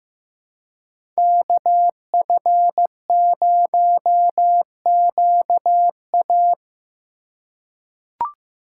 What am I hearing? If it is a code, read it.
Morse code, heard as a keyed tone, about 15 wpm, 700 Hz. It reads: KF0QA
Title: USA Call Signs - 6x Character Spacing - 15wpm